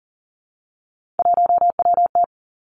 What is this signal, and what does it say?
Morse code, heard as a keyed tone, about 38 wpm, 690 Hz. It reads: 1WT
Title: Three Challenging Letter-Number Combinations 40wpm